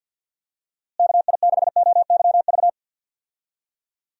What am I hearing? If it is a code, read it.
Morse code, heard as a keyed tone, about 52 wpm, 700 Hz. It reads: KI6OYV
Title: USA Call Signs - 6 Characters - Repeated 2x - 50wpm